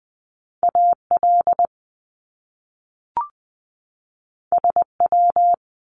Morse code, heard as a keyed tone, about 20 wpm, 700 Hz. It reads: AL SW
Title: Top Two Letter Combinations in Words 20wpm